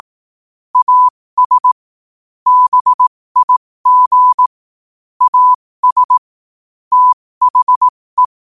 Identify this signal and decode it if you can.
Morse code, heard as a keyed tone, about 18 wpm, spaced as for 15 wpm, 1000 Hz. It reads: AS BIG AS THE
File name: qotdfiles / 15wpm